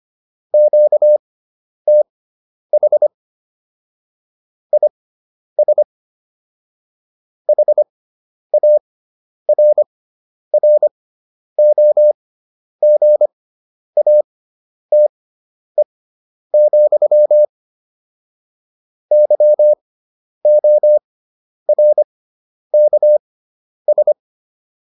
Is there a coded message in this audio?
Yes: Morse code from 0.5 to 24.1 s, heard as a keyed tone, about 25 wpm, spaced as for 10 wpm, 600 Hz. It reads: QTH IS HARROGATE, YORKS